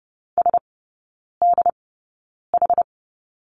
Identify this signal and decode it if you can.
Morse code, heard as a keyed tone, about 30 wpm, spaced as for 9 wpm, 700 Hz. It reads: SDH
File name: Lesson 18e - Letter m - Three Characters 30wpm